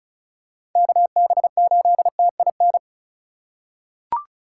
Morse code, heard as a keyed tone, about 34 wpm, 700 Hz. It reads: KB8TIN
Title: USA Call Signs - 6 Characters - 4x Character Spacing - 35wpm